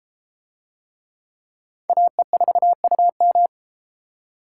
Morse code, heard as a keyed tone, about 32 wpm, 700 Hz. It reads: AE4UM